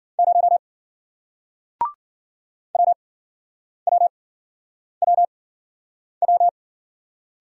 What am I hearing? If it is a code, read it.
Morse code, heard as a keyed tone, about 53 wpm, spaced as for 9 wpm, 700 Hz. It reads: 0 WWWW